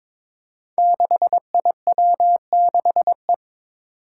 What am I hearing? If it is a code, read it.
Morse code, heard as a keyed tone, about 22 wpm, 700 Hz. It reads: 6IW6E
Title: Five Challenging Letter-Number Combinations 22wpm